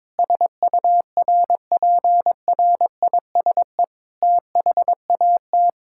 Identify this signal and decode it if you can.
Morse code, heard as a keyed tone, about 22 wpm, 700 Hz. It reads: SURPRISE THAT
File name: practice / 22wpm / Sentences with 8-Word Phrases - 22wpm